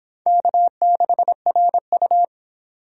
Morse code, heard as a keyed tone, about 28 wpm, 700 Hz. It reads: K6RU